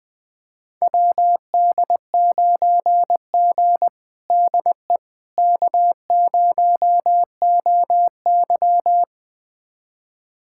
Morse code, heard as a keyed tone, about 21 wpm, 700 Hz. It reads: WD9G DE K0OY
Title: USA Callsign DE Callsign - 4 Characters - 20wpm